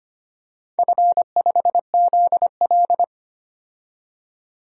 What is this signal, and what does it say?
Morse code, heard as a keyed tone, about 25 wpm, 700 Hz. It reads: F5ZL
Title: DXCC-100 Call Signs - 4 Characters - 25wpm